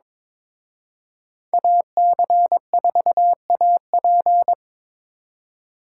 Morse code, heard as a keyed tone, about 26 wpm, 700 Hz. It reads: AC4AP
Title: Speed Racing - USA 2x2 Call Signs 22wpm